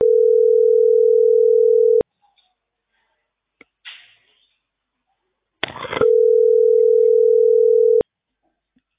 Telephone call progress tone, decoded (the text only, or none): ringback tone